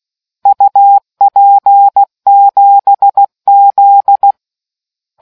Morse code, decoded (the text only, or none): UP7Z